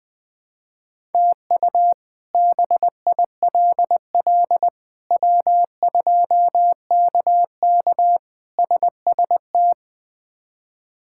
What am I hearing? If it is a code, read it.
Morse code, heard as a keyed tone, about 20 wpm, 700 Hz. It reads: TU BILL W2KK SST
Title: K1USN SST QSO Practice - 20wpm